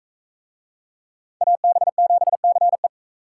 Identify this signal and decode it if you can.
Morse code, heard as a keyed tone, about 44 wpm, 700 Hz. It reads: AB7CE